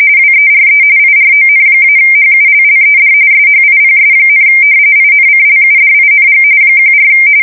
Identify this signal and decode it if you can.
RTTY, about 45.45 baud. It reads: WERE IRON-HULLED SHIPS, INCLUDING 14
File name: fsk170 / usnavy3